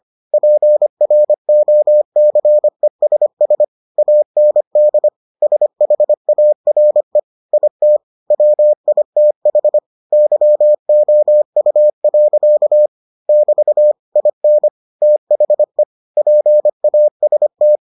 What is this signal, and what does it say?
Morse code, heard as a keyed tone, about 25 wpm, 600 Hz. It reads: PROCESS AND SHARE IT WITH YOU. <BT> IN THE PAST